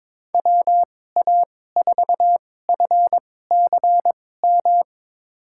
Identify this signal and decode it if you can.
Morse code, heard as a keyed tone, about 22 wpm, spaced as for 16 wpm, 700 Hz. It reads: WA4FCM
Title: USA Call Signs - 6 Characters - 2x Character Spacing - 22wpm